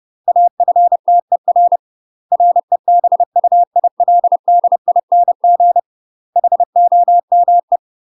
Morse code, heard as a keyed tone, about 30 wpm, 700 Hz. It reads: AFTER REBUILDING HOME
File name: UK MASTER QSOs TRAINING / 30WPM / UK MASTER QSOs 0016